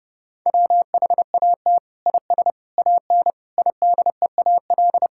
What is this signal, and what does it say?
Morse code, heard as a keyed tone, about 32 wpm, 700 Hz. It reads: WHAT IS AN IDEAL
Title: Sentences from Top 1600 Words 30wpm